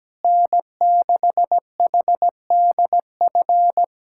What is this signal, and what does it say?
Morse code, heard as a keyed tone, about 17 wpm, 700 Hz. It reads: N6HDF
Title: USA Callsign DE Callsign 17wpm